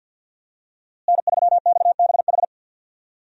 Morse code, heard as a keyed tone, about 53 wpm, 700 Hz. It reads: N3XBH
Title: USA Call Signs - 5 Characters - Repeated 2x - 50wpm